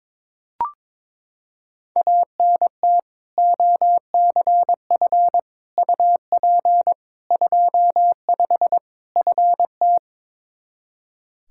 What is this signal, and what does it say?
Morse code, heard as a keyed tone, about 21 wpm, 700 Hz. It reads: ANT OCF UP 25 FT
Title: Practice QSO - Exchange 2 - Info - 22wpm